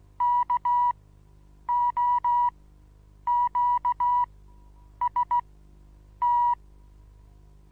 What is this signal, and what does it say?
Morse code, heard as a keyed tone, about 16 wpm, spaced as for 8 wpm, 970 Hz. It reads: KOQST
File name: Onlinelehrgang / morsen / DJ4UF Morsen25